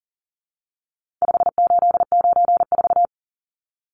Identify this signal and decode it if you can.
Morse code, heard as a keyed tone, about 40 wpm, 700 Hz. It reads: 5894
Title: Numbers - 4-Digit 40wpm